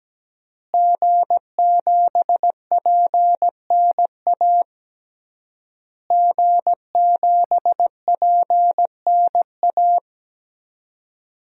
Morse code, heard as a keyed tone, about 17 wpm, 700 Hz. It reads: G7PNA G7PNA
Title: DXCC-100 Call Signs - 5 Characters - Repeated 3x - 17wpm